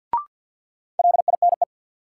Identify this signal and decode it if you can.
Morse code, heard as a keyed tone, about 45 wpm, 700 Hz. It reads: LINE
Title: Sentences from Top 1200 Words - New Word Drill Alt - 50wpm